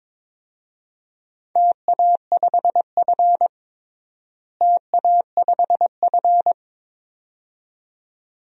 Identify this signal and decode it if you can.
Morse code, heard as a keyed tone, about 22 wpm, 700 Hz. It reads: TA5F TA5F
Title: DXCC-100 Call Signs - 4 Characters - Repeated 2x - 22wpm